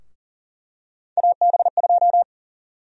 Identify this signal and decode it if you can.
Morse code, heard as a keyed tone, about 40 wpm, 700 Hz. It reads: AB2
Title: USA Call Sign Prefixes - 40wpm